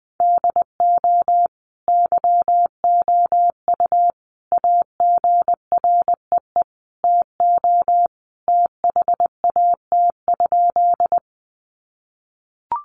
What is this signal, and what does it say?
Morse code, heard as a keyed tone, about 20 wpm, 700 Hz. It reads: DO YOU AGREE TO THAT?
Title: Sentences from Top 1100 Words 20wpm